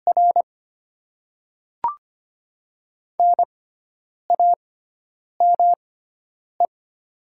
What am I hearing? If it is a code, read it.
Morse code, heard as a keyed tone, about 25 wpm, spaced as for 9 wpm, 700 Hz. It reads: R NAME
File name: Top 20 QSO Elements - 6x Character Spacing - 25wpm